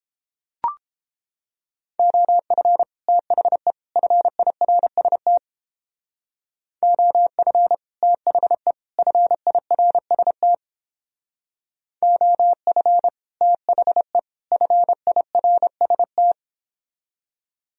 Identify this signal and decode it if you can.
Morse code, heard as a keyed tone, about 30 wpm, 700 Hz. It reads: OF THE FIRST OF THE FIRST OF THE FIRST